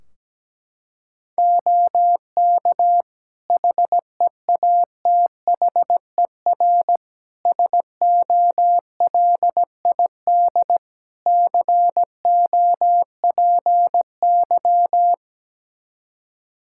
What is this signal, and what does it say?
Morse code, heard as a keyed tone, about 17 wpm, 700 Hz. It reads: OK HEATHER SOLID COPY
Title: K1USN SST QSO Practice 17wpm